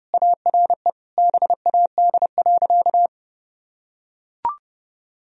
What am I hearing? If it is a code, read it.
Morse code, heard as a keyed tone, about 32 wpm, 700 Hz. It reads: ARE BAD.
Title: Sentences from Top 1000 Words - Review Part 2 - 30wpm